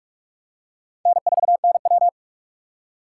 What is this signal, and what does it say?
Morse code, heard as a keyed tone, about 47 wpm, 700 Hz. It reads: N4NW
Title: K1USN SST QSO Practice - 45wpm